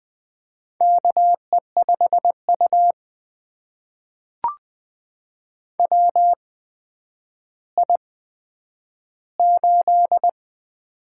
Morse code, heard as a keyed tone, about 20 wpm, spaced as for 10 wpm, 700 Hz. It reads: KE5U WI8
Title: USA Call Signs - 4 Characters - 8x Character Spacing - 20wpm